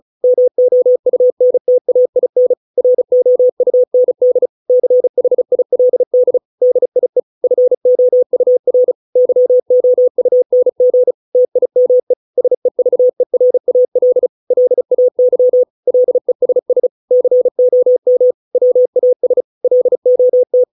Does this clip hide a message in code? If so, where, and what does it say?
Yes: Morse code from 0.2 to 20.6 s, heard as a keyed tone, about 34 wpm, 500 Hz. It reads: MOUNTAIN ROUND CHILD DIE FOUR YOUNG TIME SEVERAL LAY LESS COM WAS LOT